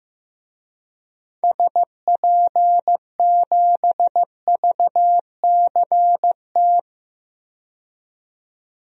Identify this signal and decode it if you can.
Morse code, heard as a keyed tone, about 15 wpm, 700 Hz. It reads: SP7VCT
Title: DXCC-100 Call Signs - 6 Characters - 15wpm